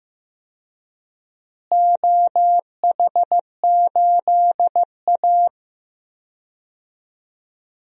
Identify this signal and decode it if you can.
Morse code, heard as a keyed tone, about 15 wpm, 700 Hz. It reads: OH8A
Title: DXCC-100 Call Signs - 4 Characters - Repeated 2x - 15wpm